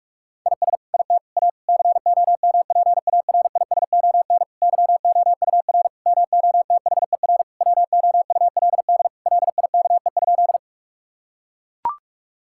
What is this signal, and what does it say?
Morse code, heard as a keyed tone, about 42 wpm, 700 Hz. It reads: IS IT A COMPARISON YOUR MOTHER WOULD LIKE?
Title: Sentences from Top 1600 Words 45wpm